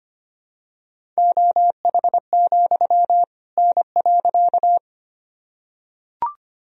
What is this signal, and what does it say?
Morse code, heard as a keyed tone, about 25 wpm, 700 Hz. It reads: OH, N.
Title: Sentences from Top 500 Words and Abbreviations 25wpm